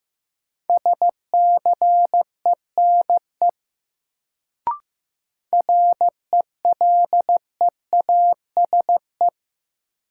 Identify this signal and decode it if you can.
Morse code, heard as a keyed tone, about 15 wpm, 700 Hz. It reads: SCENE RELEASE
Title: Sentences from Top 1100 Words - New Word Drill Alt - 15wpm